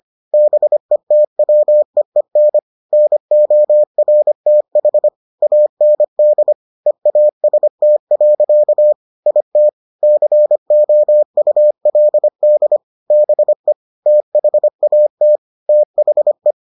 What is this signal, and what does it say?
Morse code, heard as a keyed tone, about 25 wpm, 600 Hz. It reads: BETWEEN NORTH AND EAST. IT COULD BE THAT THE